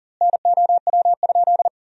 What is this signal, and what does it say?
Morse code, heard as a keyed tone, about 40 wpm, 700 Hz. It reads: NOW?
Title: 2x Word-Spacing - Sentences from Top 400 Words - 40wpm